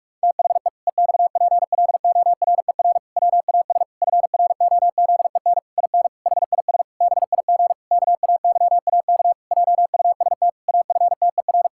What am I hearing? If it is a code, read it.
Morse code, heard as a keyed tone, about 46 wpm, 700 Hz. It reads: THE EXPLORER WAS FROZEN IN HIS BIG KAYAK JUST AFTER
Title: Pangram Challenge v2 - 45wpm